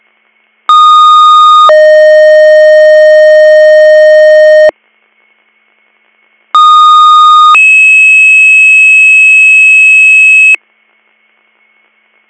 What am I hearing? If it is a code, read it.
Two-tone page, heard as a one-second tone, then a three-second tone, about 1230 then 620 Hz, about 1230 then 2600 Hz. It is heard from 0.7 to 10.5 s.